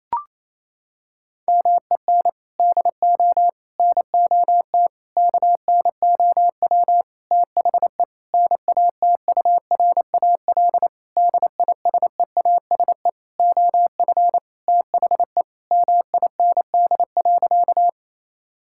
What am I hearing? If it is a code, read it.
Morse code, heard as a keyed tone, about 28 wpm, 700 Hz. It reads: MEN DO NOT KNOW THE NATURAL DISEASE OF THE MIND.
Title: Sentences from Top 1000 Words - Review Part 6 - 28wpm